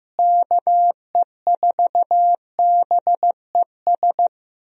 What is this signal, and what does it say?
Morse code, heard as a keyed tone, about 15 wpm, 700 Hz. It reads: KE4BES
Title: USA Call Signs - 6 Characters - Repeated 2x - 15wpm